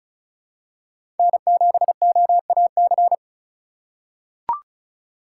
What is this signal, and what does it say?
Morse code, heard as a keyed tone, about 36 wpm, 700 Hz. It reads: N7OAC
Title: USA Call Signs - 5 Characters - 2x Character Spacing - 35wpm